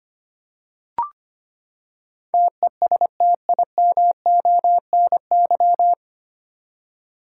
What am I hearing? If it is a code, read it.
Morse code, heard as a keyed tone, about 25 wpm, 700 Hz. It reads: TESTIMONY
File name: Nine Letter Words - 25wpm